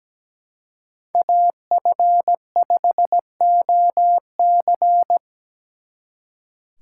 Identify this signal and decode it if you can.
Morse code, heard as a keyed tone, about 20 wpm, 700 Hz. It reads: AF5OC
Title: Speed Racing - USA 2x2 Call Signs 17wpm